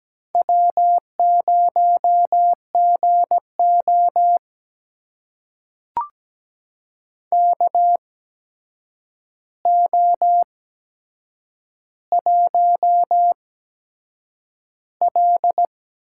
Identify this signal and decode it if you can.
Morse code, heard as a keyed tone, about 17 wpm, spaced as for 7 wpm, 700 Hz. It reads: W0GO KO1L
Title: USA Call Signs - 4 Characters - 8x Character Spacing - 17wpm